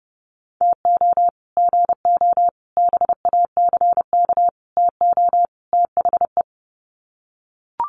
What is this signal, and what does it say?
Morse code, heard as a keyed tone, about 30 wpm, 700 Hz. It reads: TO GO BACK TO THE
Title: Sets of 5 Words 30wpm